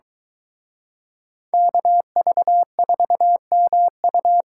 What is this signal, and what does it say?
Morse code, heard as a keyed tone, about 23 wpm, 700 Hz. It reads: KV4MU